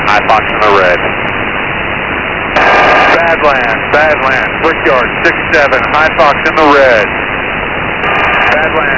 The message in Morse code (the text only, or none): none